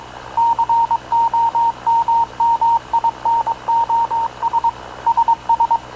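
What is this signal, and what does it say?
Morse code, heard as a keyed tone, about 22 wpm, 930 Hz. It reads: COMMINOS SS
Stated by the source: Havana Radio / CLA sending Spanish press (PX) in Morse, recorded by Radio Officer Paul Dumesnil onboard CNAV QUEST/CZDO.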